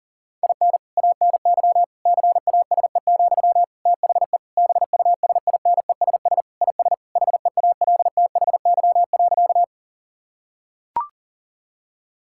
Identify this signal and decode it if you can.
Morse code, heard as a keyed tone, about 40 wpm, 700 Hz. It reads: IN ANY CASE, THE BUSINESS IS HEALTHY.
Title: Sentences from Top 1000 Words - Review Part 5 - 40wpm